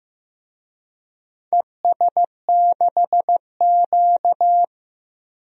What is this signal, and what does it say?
Morse code, heard as a keyed tone, about 15 wpm, 700 Hz. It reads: ES6Q
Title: DXCC-100 Call Signs - 4 Characters - 15wpm